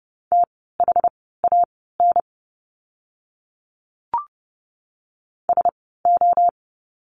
Morse code, heard as a keyed tone, about 30 wpm, spaced as for 17 wpm, 700 Hz. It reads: THAN SO
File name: course / 30:10wpm / 230 Lesson 37f - Letter x - Words 30wpm